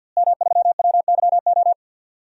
Morse code, heard as a keyed tone, about 52 wpm, 700 Hz. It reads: M3WYO
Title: DXCC-100 Call Signs - 5 Characters - Repeated 3x - 50wpm